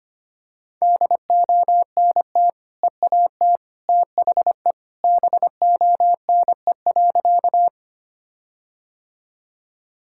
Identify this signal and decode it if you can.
Morse code, heard as a keyed tone, about 25 wpm, 700 Hz. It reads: DONT EAT THE BONE.
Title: Sentences from Top 1600 Words 25wpm